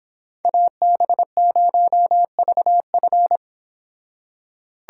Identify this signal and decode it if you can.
Morse code, heard as a keyed tone, about 26 wpm, 700 Hz. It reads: AB0VF